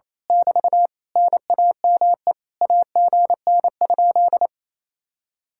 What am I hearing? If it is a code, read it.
Morse code, heard as a keyed tone, about 27 wpm, 700 Hz. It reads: <BT> NAME AGN?
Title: Speed-Racing Common QSO Phrases - 28wpm